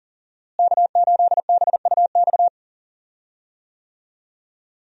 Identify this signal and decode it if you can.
Morse code, heard as a keyed tone, about 40 wpm, 700 Hz. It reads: K8BUX
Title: USA Call Signs - 5 Characters - Repeated 3x - 40wpm